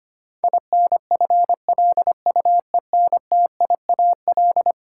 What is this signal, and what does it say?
Morse code, heard as a keyed tone, about 25 wpm, 700 Hz. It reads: INFLUENTIAL
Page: morse-code-ninja-practice-25wpm